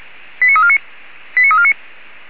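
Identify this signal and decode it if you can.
Five-tone selective call (ZVEI), heard as quick runs of five tones, tones of about 70 ms. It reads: 89349 89349
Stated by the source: ZVEI five-tone selective call 89347 received via SDR.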